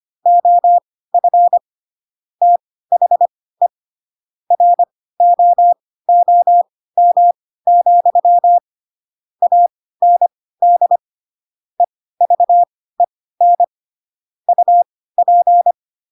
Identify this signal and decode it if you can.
Morse code, heard as a keyed tone, about 25 wpm, spaced as for 16 wpm, 700 Hz. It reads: OF THE ROOM, AND EVEN UP
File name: Pinocchio / Pinocchio ch1 10 25 ik1hge